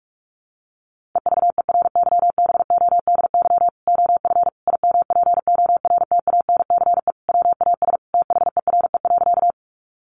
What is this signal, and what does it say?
Morse code, heard as a keyed tone, about 44 wpm, 700 Hz. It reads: EVERYBODY OF IMPORTANCE WAS THERE.
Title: Sentences from Top 1400 Words 45wpm